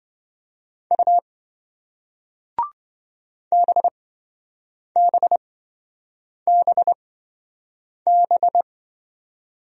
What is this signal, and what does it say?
Morse code, heard as a keyed tone, about 26 wpm, spaced as for 7 wpm, 700 Hz. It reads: U BBBB